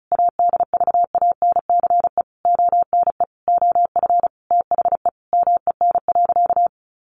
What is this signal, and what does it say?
Morse code, heard as a keyed tone, about 34 wpm, 700 Hz. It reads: ADVANCE ONE OF THE MEN.
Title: Sentences from Top 1400 Words 35wpm